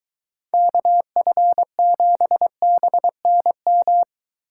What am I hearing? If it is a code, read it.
Morse code, heard as a keyed tone, about 22 wpm, 700 Hz. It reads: KF7BNM